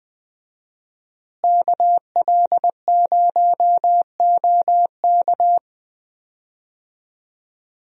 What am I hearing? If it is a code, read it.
Morse code, heard as a keyed tone, about 20 wpm, 700 Hz. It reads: KL0OK
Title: USA Call Signs - 5 Characters - Repeated 2x - 20wpm